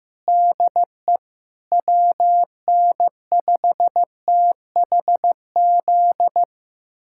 Morse code, heard as a keyed tone, about 15 wpm, 700 Hz. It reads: DE WN5THZ
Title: USA Callsign DE Callsign - 6 Characters - 15wpm